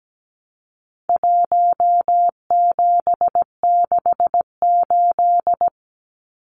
Morse code, heard as a keyed tone, about 17 wpm, 700 Hz. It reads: 1768
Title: Numbers - Contest Style - 4-Digit 17wpm